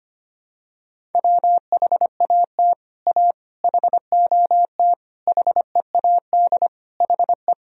Morse code, heard as a keyed tone, about 25 wpm, 700 Hz. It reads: WHAT A HOT HEAD HE
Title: Sentences from Top 1000 Words - Review Part 3 - 25wpm